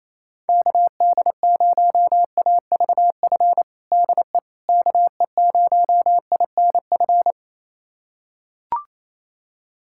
Morse code, heard as a keyed tone, about 28 wpm, 700 Hz. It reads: KD0AVF DE KE0INF
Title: USA Callsign DE Callsign 28wpm